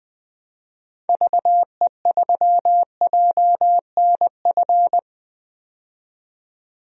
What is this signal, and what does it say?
Morse code, heard as a keyed tone, about 20 wpm, 700 Hz. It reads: VE3JNF
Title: DXCC-100 Call Signs - 6 Characters - 20wpm